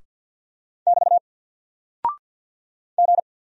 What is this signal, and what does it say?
Morse code, heard as a keyed tone, about 48 wpm, 700 Hz. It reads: <BT> G
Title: Single Character (Letter:Number:Punctuation:Prosign) Alt 50wpm